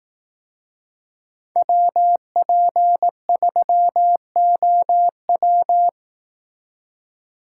Morse code, heard as a keyed tone, about 18 wpm, 700 Hz. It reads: WP3OW
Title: Speed Racing - USA 2x2 Call Signs 15wpm